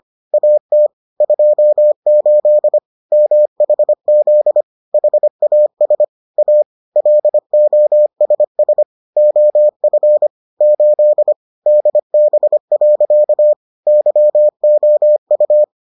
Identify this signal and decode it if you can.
Morse code, heard as a keyed tone, about 25 wpm, 600 Hz. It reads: AT 28 MHZ HAS A LOSS OF 8 DB. YOU